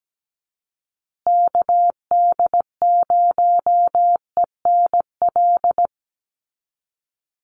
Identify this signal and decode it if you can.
Morse code, heard as a keyed tone, about 17 wpm, 700 Hz. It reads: KD0ENL
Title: USA Call Signs Repeated 3x 17wpm